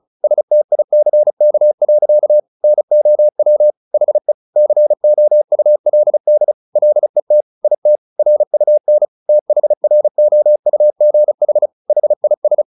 Morse code, heard as a keyed tone, about 34 wpm, 600 Hz. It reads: STICK. NOW HE COULD LET IT RUN THROUGH HIS
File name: the old man and the sea / the old man and the sea 1 35wpm